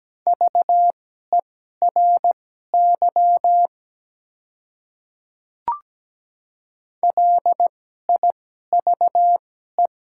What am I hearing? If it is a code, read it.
Morse code, heard as a keyed tone, about 17 wpm, spaced as for 12 wpm, 700 Hz. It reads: VERY LIVE
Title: 2x Character Spacing - Top 200 Words - 17wpm